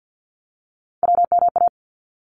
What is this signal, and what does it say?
Morse code, heard as a keyed tone, about 50 wpm, 700 Hz. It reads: WMA
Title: ARRL Sections 50wpm